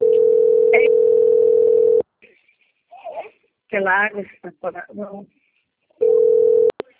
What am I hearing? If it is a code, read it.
Ringback tone (North America), 440 + 480 Hz, with speech.
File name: out- - - - - .45651